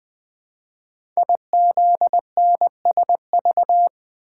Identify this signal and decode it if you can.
Morse code, heard as a keyed tone, about 20 wpm, 700 Hz. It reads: IZNSV
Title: Code Groups - 5 Letters - 20wpm